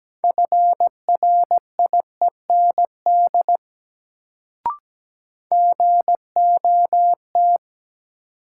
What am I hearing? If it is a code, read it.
Morse code, heard as a keyed tone, about 17 wpm, 700 Hz. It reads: FRIEND GOT
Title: Sentences from Top 300 Words - New Word Drill - 17wpm